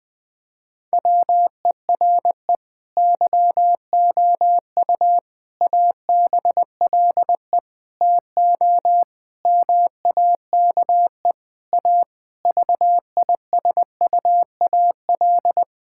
Morse code, heard as a keyed tone, about 20 wpm, 700 Hz. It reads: WERE YOU ABLE TO MAKE A VISUAL
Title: Sentences from Top 1000 Words - Review Part 6 - 20wpm